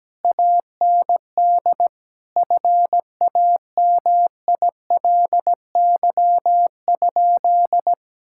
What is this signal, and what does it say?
Morse code, heard as a keyed tone, about 17 wpm, 700 Hz. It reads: AND FAMILY?
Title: Sentences from Top 1600 Words 17wpm